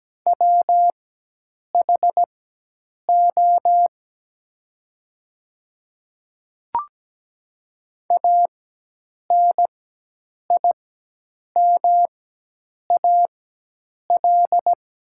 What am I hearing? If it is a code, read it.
Morse code, heard as a keyed tone, about 17 wpm, spaced as for 8 wpm, 700 Hz. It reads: WHO ANIMAL
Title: 4x Character Spacing - Top 200 Words - 17wpm